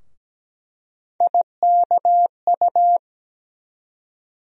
Morse code, heard as a keyed tone, about 17 wpm, 700 Hz. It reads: IKU
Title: USA Call Sign Postfixes - 3 Letters 17wpm